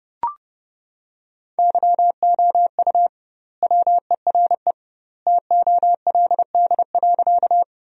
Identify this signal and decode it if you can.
Morse code, heard as a keyed tone, about 30 wpm, 700 Hz. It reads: YOU WERE TOLD.
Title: 2x Word-Spacing - Sentences from Top 400 Words - 30wpm